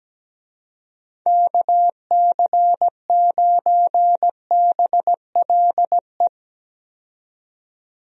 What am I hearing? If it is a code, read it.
Morse code, heard as a keyed tone, about 17 wpm, 700 Hz. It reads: KC9BLE
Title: USA Call Signs - 6 Characters - 17wpm